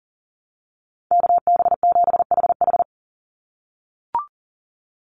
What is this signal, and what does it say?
Morse code, heard as a keyed tone, about 40 wpm, 700 Hz. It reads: KB7HH
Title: NA SOTA Chaser Callsigns 40wpm